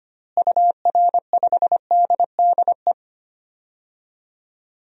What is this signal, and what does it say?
Morse code, heard as a keyed tone, about 25 wpm, 700 Hz. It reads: UR5DDE
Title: DXCC-100 Call Signs - 6 Characters - Repeated 3x - 25wpm